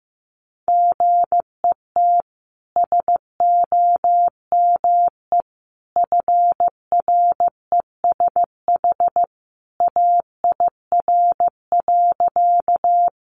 Morse code, heard as a keyed tone, about 15 wpm, 700 Hz. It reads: GET SOME FRESH AIR.
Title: Sentences from Top 1200 Words 15wpm